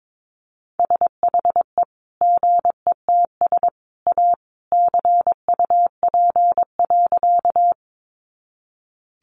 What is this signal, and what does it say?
Morse code, heard as a keyed tone, about 22 wpm, 700 Hz. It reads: SHE GETS A CUP.
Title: Easy Phrases 22wpm